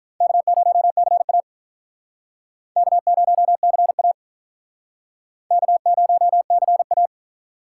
Morse code, heard as a keyed tone, about 44 wpm, 700 Hz. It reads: K0CA K0CA K0CA